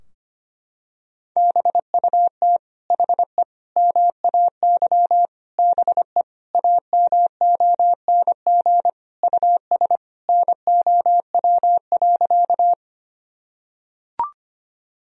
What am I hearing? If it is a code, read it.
Morse code, heard as a keyed tone, about 25 wpm, 700 Hz. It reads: BUT HE MAY BE AMONG US NOW.
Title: Sentences from Top 1000 Words - Review Part 3 - 25wpm